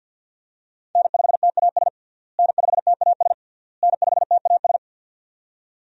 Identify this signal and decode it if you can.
Morse code, heard as a keyed tone, about 53 wpm, 700 Hz. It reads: N5TAS N5TAS N5TAS
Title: USA Call Signs - 5 Characters - Repeated 3x - 50wpm